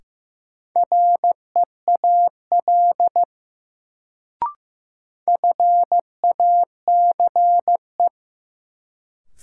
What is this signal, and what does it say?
Morse code, heard as a keyed tone, about 15 wpm, 700 Hz. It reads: REAL FACE
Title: Sentences from Top 300 Words - New Word Drill - 15wpm